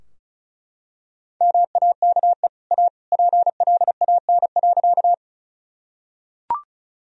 Morse code, heard as a keyed tone, about 36 wpm, 700 Hz. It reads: MAKE A PLAN.
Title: Sentences from Top 1000 Words - Review Part 2 - 35wpm